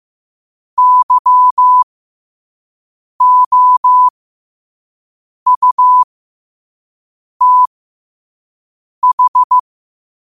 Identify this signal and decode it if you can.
Morse code, heard as a keyed tone, about 15 wpm, spaced as for 5 wpm, 1000 Hz. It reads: YOUTH
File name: qotdfiles / 5wpm